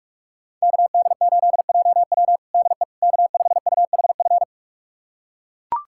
Morse code, heard as a keyed tone, about 42 wpm, 700 Hz. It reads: KD8JW DE K5UHF
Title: USA Callsign DE Callsign - 5 Characters - 45wpm